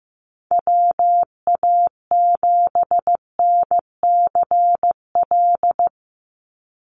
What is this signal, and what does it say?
Morse code, heard as a keyed tone, about 15 wpm, 700 Hz. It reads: WA7NCL
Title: NA SOTA Activator Callsigns 15wpm